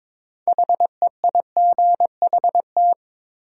Morse code, heard as a keyed tone, about 22 wpm, 700 Hz. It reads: HEIGHT
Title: Sentences from Top 1600 Words - New Word Drill Alt - 22wpm